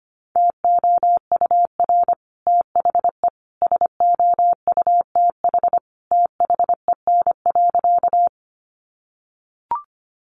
Morse code, heard as a keyed tone, about 25 wpm, 700 Hz. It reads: TOUR THE SOUTH THEN.
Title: Sentences from Top 1400 Words 25wpm